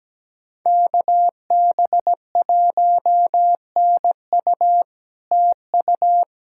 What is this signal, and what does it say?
Morse code, heard as a keyed tone, about 17 wpm, 700 Hz. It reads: KB1NU TU
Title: SOTA QSO Practice 17wpm